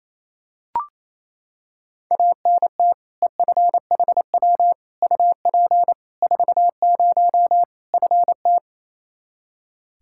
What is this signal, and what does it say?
Morse code, heard as a keyed tone, about 28 wpm, 700 Hz. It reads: ANT EFHW UP 40 FT
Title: Practice QSO - Exchange 2 - Info - 28wpm